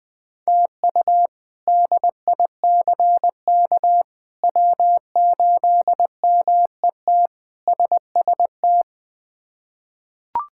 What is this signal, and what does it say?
Morse code, heard as a keyed tone, about 20 wpm, 700 Hz. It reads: TU DICK W8MET SST
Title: K1USN SST QSO Practice - 20wpm